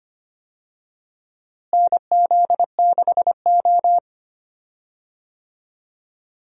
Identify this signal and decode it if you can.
Morse code, heard as a keyed tone, about 25 wpm, 700 Hz. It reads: NZ6O
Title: USA Call Signs - 4 Characters - Repeated 3x - 25wpm